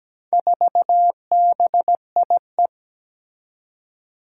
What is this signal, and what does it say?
Morse code, heard as a keyed tone, about 17 wpm, 700 Hz. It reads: 4BIE
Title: Four Challenging Letter-Number Combinations 17wpm